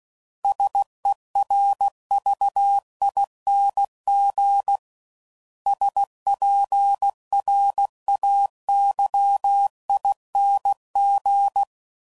Morse code, heard as a keyed tone, about 16 wpm, 800 Hz. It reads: SERVING SPRAYING
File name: Suffixes / Special INGs / Sngl Ings